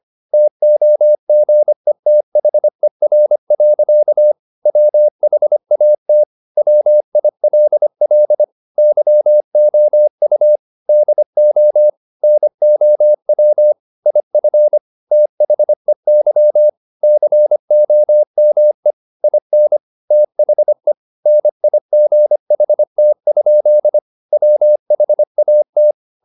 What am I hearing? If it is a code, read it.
Morse code, heard as a keyed tone, about 25 wpm, 600 Hz. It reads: TOGETHER. WHAT WILL YOU DO NOW IF THEY COME IN THE NIGHT? WHAT